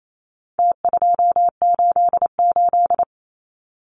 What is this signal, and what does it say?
Morse code, heard as a keyed tone, about 28 wpm, 700 Hz. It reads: T288
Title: Numbers - Contest Style - 4-Digit 28wpm